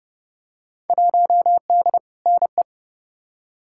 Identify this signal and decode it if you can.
Morse code, heard as a keyed tone, about 30 wpm, 700 Hz. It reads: 1D NE
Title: Field Day Exchanges - 30wpm